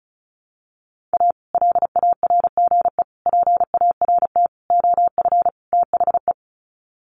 Morse code, heard as a keyed tone, about 34 wpm, 700 Hz. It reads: A LARGE PART OF THE
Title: Sets of 5 Words Encore2 35wpm